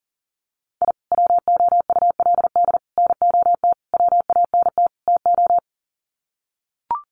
Morse code, heard as a keyed tone, about 40 wpm, 700 Hz. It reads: I WOULD NOT WANT TO
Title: Sets of 5 Words 40wpm